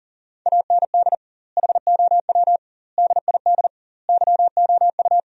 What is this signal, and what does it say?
Morse code, heard as a keyed tone, about 38 wpm, 700 Hz. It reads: AND HOW DID YOU
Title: 2x Word-Spacing - Sentences from Top 100 Words - 40wpm